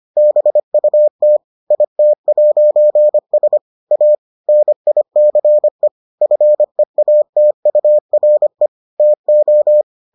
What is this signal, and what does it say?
Morse code, heard as a keyed tone, about 25 wpm, 600 Hz. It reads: BUT IT'S A NICE FEATURE TO